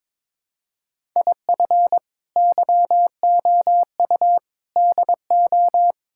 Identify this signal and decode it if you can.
Morse code, heard as a keyed tone, about 22 wpm, 700 Hz. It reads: IF YOU DO
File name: practice / 22wpm / Build A Phrase - 5-Word Phrases - 22wpm